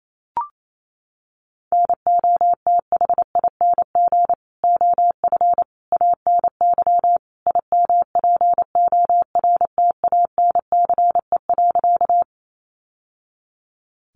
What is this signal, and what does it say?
Morse code, heard as a keyed tone, about 28 wpm, 700 Hz. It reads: NOTHING OF ANY IMPORTANCE.
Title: Sentences from Top 1200 Words 28wpm